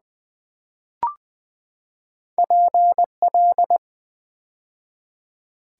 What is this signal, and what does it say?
Morse code, heard as a keyed tone, about 20 wpm, 700 Hz. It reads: PL
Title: Build A Word - 2 Letters At A Time - 4-Letter Words - 20wpm